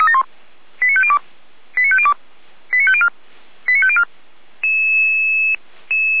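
Five-tone selective call (ZVEI), heard as quick runs of five tones, tones of about 80 ms. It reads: 89502 89502 89504 89504
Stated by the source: ZVEI five-tone selective call received with an SDR.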